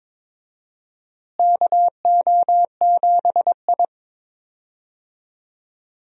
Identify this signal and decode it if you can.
Morse code, heard as a keyed tone, about 22 wpm, 700 Hz. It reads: KO7I
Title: USA Call Signs - 4 Characters - Repeated 2x - 22wpm